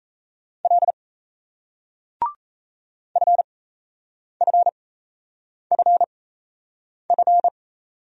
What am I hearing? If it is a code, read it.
Morse code, heard as a keyed tone, about 39 wpm, spaced as for 8 wpm, 700 Hz. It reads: L FFFF